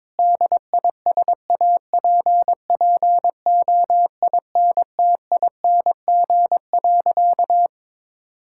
Morse code, heard as a keyed tone, about 22 wpm, 700 Hz. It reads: DISAPPOINTING.
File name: practice / 22wpm / Sentences with 4-Word Phrases - 22wpm